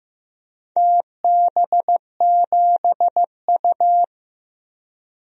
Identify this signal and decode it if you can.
Morse code, heard as a keyed tone, about 15 wpm, 700 Hz. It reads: TB7U
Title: DXCC-100 Call Signs - 4 Characters - 15wpm